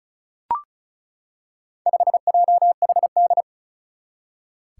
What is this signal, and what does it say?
Morse code, heard as a keyed tone, about 36 wpm, 700 Hz. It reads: 5JHD
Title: Four Challenging Letter-Number Combinations 35wpm